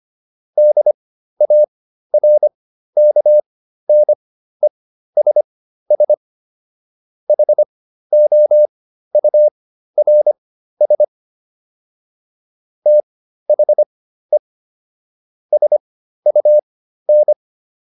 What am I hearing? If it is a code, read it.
Morse code, heard as a keyed tone, about 25 wpm, spaced as for 13 wpm, 600 Hz. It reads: DARKNESS HOURS THE SUN